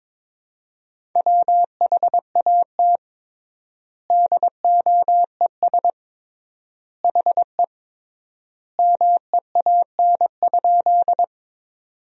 Morse code, heard as a keyed tone, about 22 wpm, 700 Hz. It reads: WHAT DOES HE MEAN?
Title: 3x Word-Spacing - Sentences from Top 200 Words - 22wpm